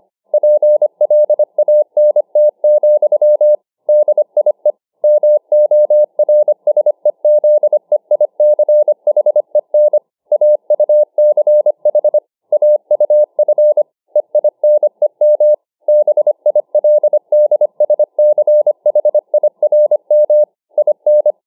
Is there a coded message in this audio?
Yes: Morse code from 0.3 to 21.3 s, heard as a keyed tone, about 25 wpm, 600 Hz. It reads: PLANT, DIE MORSEZEICHEN AUCH AUF EINEM BILDSCHIRM IN